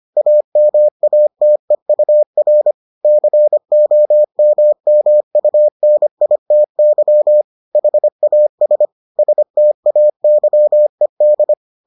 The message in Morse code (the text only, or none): AMATEUR COMMUNITY HAS STAYED